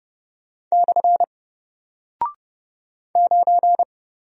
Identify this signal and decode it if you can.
Morse code, heard as a keyed tone, about 30 wpm, 700 Hz. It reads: / 9
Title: Single Character (Letter:Number:Punctuation:Prosign) Alt 30wpm